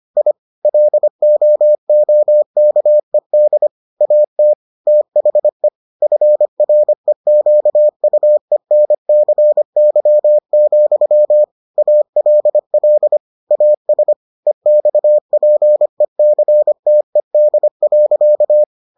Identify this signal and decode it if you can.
Morse code, heard as a keyed tone, about 25 wpm, 600 Hz. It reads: I LOOKED AT THE FREQUENCY, ALL AS EXPECTED.